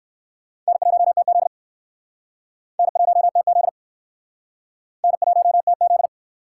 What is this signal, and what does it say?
Morse code, heard as a keyed tone, about 57 wpm, 700 Hz. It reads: N1TZ N1TZ N1TZ